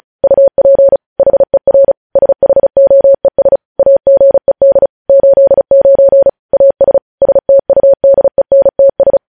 Morse code, heard as a keyed tone, about 36 wpm, 550 Hz. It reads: UP HER SHOES AGED 89 AS STUDENTS
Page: News Headlines in Morse Code at 35 WPM